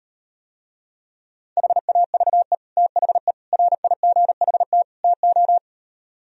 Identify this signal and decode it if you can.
Morse code, heard as a keyed tone, about 38 wpm, 700 Hz. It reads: HAVE THE RIGHT TO